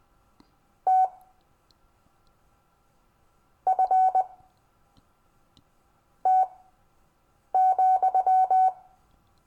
Morse code, heard as a keyed tone, about 20 wpm, spaced as for 4 wpm, 750 Hz. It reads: TFT,
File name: cw / Lesson15 five forwardslash